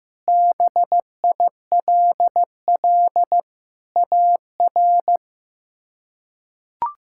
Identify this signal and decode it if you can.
Morse code, heard as a keyed tone, about 15 wpm, 700 Hz. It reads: BILL AR
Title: K1USN SST QSO Practice - 15wpm